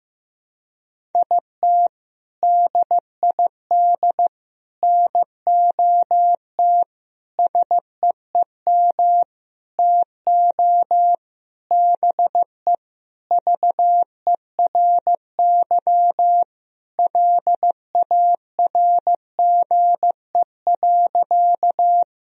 Morse code, heard as a keyed tone, about 15 wpm, 700 Hz. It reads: IT DID NOT SEEM TO BE VERY LARGE.
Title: Sentences from Top 1000 Words - Review Part 1 - 15wpm